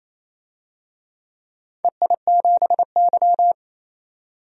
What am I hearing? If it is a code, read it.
Morse code, heard as a keyed tone, about 28 wpm, 700 Hz. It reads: EI7Y
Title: DXCC-100 Call Signs - 4 Characters - 28wpm